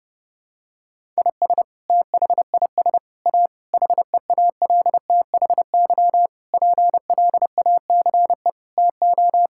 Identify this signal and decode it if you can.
Morse code, heard as a keyed tone, about 30 wpm, 700 Hz. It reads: IS THIS A HEALTHY PLACE TO
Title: Sentences from Top 1000 Words - Review Part 5 - 30wpm